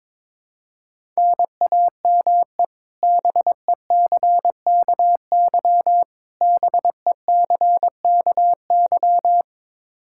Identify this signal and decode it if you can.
Morse code, heard as a keyed tone, about 22 wpm, 700 Hz. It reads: NAME BECKY BECKY
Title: K1USN SST QSO Practice 22wpm